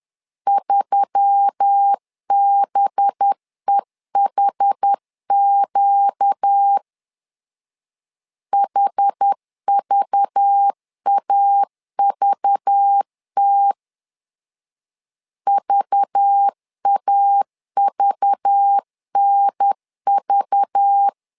Morse code, heard as a keyed tone, about 10 wpm, 800 Hz. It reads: 3BEHQ HVAVT VAVNV
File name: LEZIONI / 5 LBVC / 5 LBV 10wpm 2248